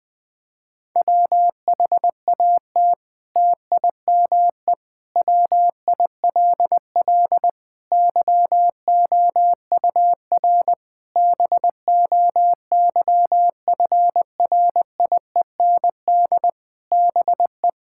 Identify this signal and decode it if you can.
Morse code, heard as a keyed tone, about 20 wpm, 700 Hz. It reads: WHAT TIME WILL YOUR BOYFRIEND BE
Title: Sentences from Top 1000 Words - Review Part 5 - 20wpm